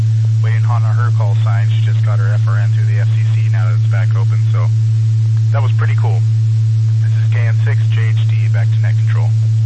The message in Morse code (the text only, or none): none